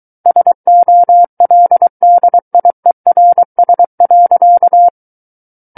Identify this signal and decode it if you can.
Morse code, heard as a keyed tone, about 23 wpm, 700 Hz. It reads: SOLDIERS.
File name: Book One / 14-Chapter 14 - 23 WPM